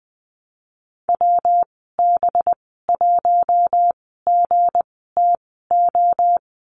Morse code, heard as a keyed tone, about 20 wpm, spaced as for 15 wpm, 700 Hz. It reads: WB1GTO
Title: 238 Lesson 38g - Letter q - Call Signs 20wpm